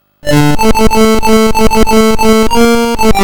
There is no code.